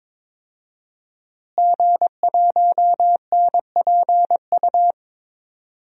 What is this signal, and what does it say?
Morse code, heard as a keyed tone, about 22 wpm, 700 Hz. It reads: G1NPU
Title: DXCC-100 Call Signs - 5 Characters - Repeated 2x - 22wpm